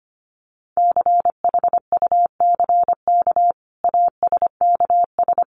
Morse code, heard as a keyed tone, about 25 wpm, 700 Hz. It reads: CHUCK ASKS
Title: Additional Easy Phrases 25wpm